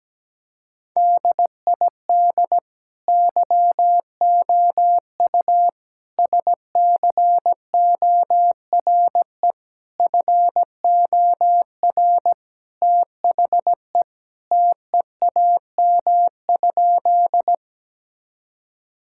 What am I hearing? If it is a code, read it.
Morse code, heard as a keyed tone, about 17 wpm, 700 Hz. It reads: DID YOU SCORE FOR THE TEAM?
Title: Sentences from Top 1600 Words 17wpm